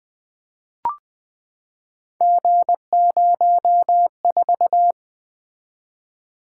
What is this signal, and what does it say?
Morse code, heard as a keyed tone, about 20 wpm, 700 Hz. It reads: G04
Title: Three Letter-Number Combinations - 20wpm